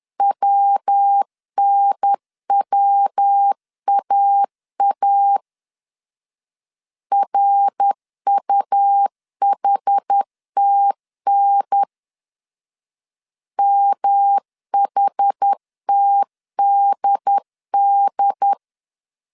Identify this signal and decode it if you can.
Morse code, heard as a keyed tone, about 10 wpm, 800 Hz. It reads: WNWAA RUHTN MHTDD
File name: CW / DET / 1 3 1500C S B 10wpm 3335